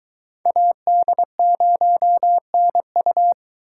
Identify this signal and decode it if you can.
Morse code, heard as a keyed tone, about 22 wpm, 700 Hz. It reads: AD0NU